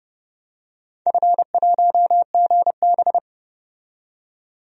Morse code, heard as a keyed tone, about 30 wpm, 700 Hz. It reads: F1GB
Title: DXCC-100 Call Signs - 4 Characters - 30wpm